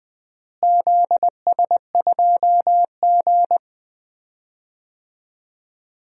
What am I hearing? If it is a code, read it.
Morse code, heard as a keyed tone, about 20 wpm, 700 Hz. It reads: ZS2G
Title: DXCC-100 Call Signs - 4 Characters - Repeated 3x - 20wpm